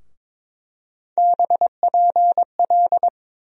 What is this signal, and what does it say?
Morse code, heard as a keyed tone, about 22 wpm, 700 Hz. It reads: BPL
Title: USA Call Sign Postfixes - 3 Letters 22wpm